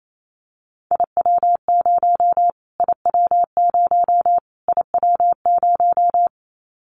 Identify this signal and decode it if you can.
Morse code, heard as a keyed tone, about 28 wpm, 700 Hz. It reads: IW0 IW0 IW0
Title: Callsign Prefixes for Top 25 DXCC - Repeated 3x - 28wpm